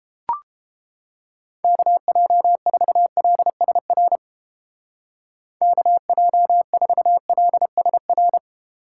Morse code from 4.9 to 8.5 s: KJ4LSR

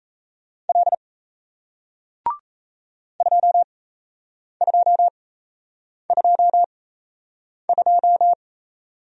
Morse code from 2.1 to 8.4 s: 2222

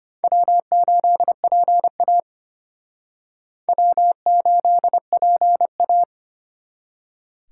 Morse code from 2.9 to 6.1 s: W8PA

